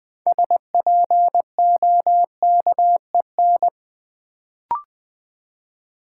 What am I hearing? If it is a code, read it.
Morse code, heard as a keyed tone, about 20 wpm, 700 Hz. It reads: SPOKEN
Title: Build A Word - 3 Letters At A Time - 6-Letter Words - 20wpm